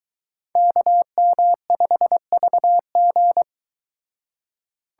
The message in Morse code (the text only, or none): KM5VG